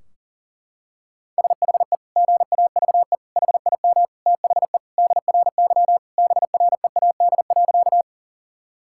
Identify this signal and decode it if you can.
Morse code, heard as a keyed tone, about 40 wpm, 700 Hz. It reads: SHE GAVE HIM THE DRY BREAD.
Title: Sentences from Top 1600 Words 40wpm